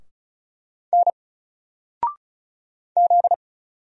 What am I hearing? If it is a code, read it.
Morse code, heard as a keyed tone, about 36 wpm, 700 Hz. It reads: N Z
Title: Single Letter-Number Alt 35wpm